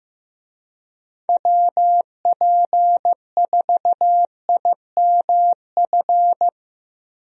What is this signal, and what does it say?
Morse code, heard as a keyed tone, about 15 wpm, 700 Hz. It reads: WP4IMF
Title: USA Call Signs - 6 Characters - Repeated 2x - 15wpm